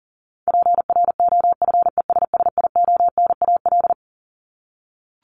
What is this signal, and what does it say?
Morse code, heard as a keyed tone, about 40 wpm, 700 Hz. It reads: PROFESSIONAL
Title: Top 501-750 Words 40wpm